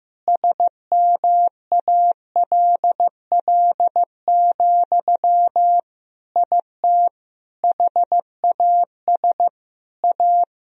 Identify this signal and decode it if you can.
Morse code, heard as a keyed tone, about 15 wpm, 700 Hz. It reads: SMALL, IT HAS A
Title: Sentences from Top 1000 Words - Review Part 3 - 15wpm